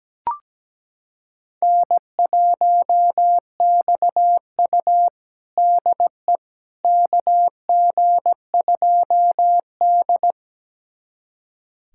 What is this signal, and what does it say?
Morse code, heard as a keyed tone, about 17 wpm, 700 Hz. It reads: N1XU DE KG2D
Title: USA Callsign DE Callsign - 4 Characters - 17wpm